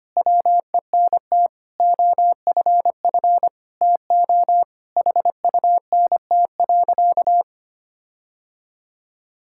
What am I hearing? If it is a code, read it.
Morse code, heard as a keyed tone, about 25 wpm, 700 Hz. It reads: WENT OFF TO HUNT.
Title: Sentences from Top 1600 Words 25wpm